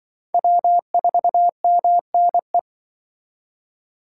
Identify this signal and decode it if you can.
Morse code, heard as a keyed tone, about 25 wpm, 700 Hz. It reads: W4MNE
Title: Speed Racing - USA 1x3 Call Signs 20wpm